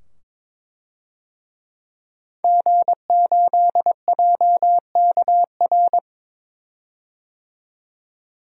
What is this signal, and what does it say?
Morse code, heard as a keyed tone, about 22 wpm, 700 Hz. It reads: G8JKR
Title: DXCC-100 Call Signs - 5 Characters - 22wpm